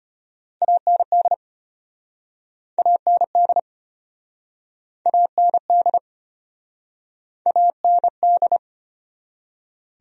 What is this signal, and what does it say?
Morse code, heard as a keyed tone, about 33 wpm, 700 Hz. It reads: AND AND AND AND